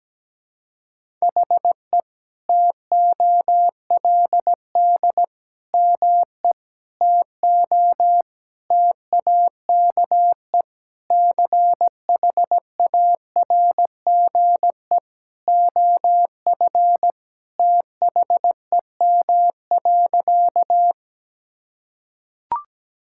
Morse code, heard as a keyed tone, about 17 wpm, 700 Hz. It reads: HE TOLD ME TO TAKE CHARGE OF THEM.
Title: Sentences from Top 1000 Words - Review Part 6 - 17wpm